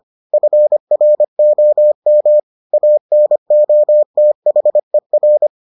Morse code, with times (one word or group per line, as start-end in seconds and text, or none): 0.3-2.4 FROM
2.7-5.5 ANOTHER